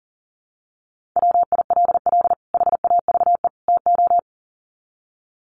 Morse code, heard as a keyed tone, about 40 wpm, 700 Hz. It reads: WILL HAVE TO
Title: Sets of 3 Words 40wpm